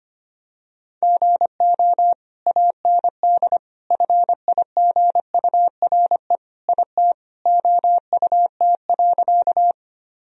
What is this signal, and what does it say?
Morse code, heard as a keyed tone, about 25 wpm, 700 Hz. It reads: GO AND FIGURE IT OUT.
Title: Sentences from Top 1000 Words - Review Part 2 - 25wpm